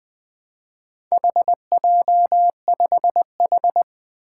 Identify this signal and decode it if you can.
Morse code, heard as a keyed tone, about 20 wpm, 700 Hz. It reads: HJ5H
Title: Four Challenging Letter-Number Combinations 20wpm